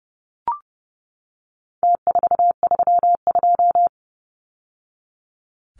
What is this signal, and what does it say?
Morse code, heard as a keyed tone, about 30 wpm, 700 Hz. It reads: T432
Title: Numbers - Contest Style - 4-Digit 30wpm